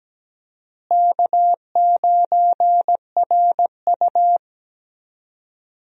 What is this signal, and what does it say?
Morse code, heard as a keyed tone, about 17 wpm, 700 Hz. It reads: K9RU
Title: USA Call Signs - 4 Characters - Repeated 2x - 17wpm